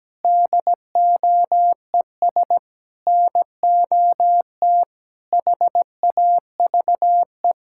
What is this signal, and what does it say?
Morse code, heard as a keyed tone, about 17 wpm, 700 Hz. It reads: DOES NOT HAVE
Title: Sentences from Top 1600 Words 17wpm